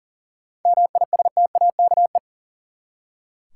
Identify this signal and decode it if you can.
Morse code, heard as a keyed tone, about 40 wpm, 700 Hz. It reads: MISTAKE
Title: Seven Letter Words - 40wpm